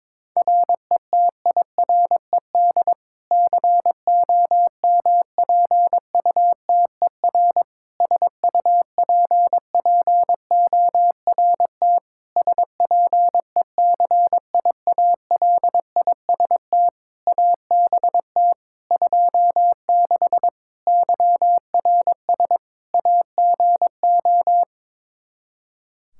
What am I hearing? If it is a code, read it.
Morse code, heard as a keyed tone, about 22 wpm, 700 Hz. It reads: RETIRED COMPUTER SUPPORT SPECIALIST ABT 26 YRS AGO
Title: Practice QSO - Exchange 3 - More Info - 22wpm